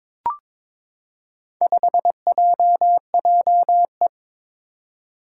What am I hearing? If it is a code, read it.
Morse code, heard as a keyed tone, about 21 wpm, 700 Hz. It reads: 5JJE
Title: Four Challenging Letter-Number Combinations 22wpm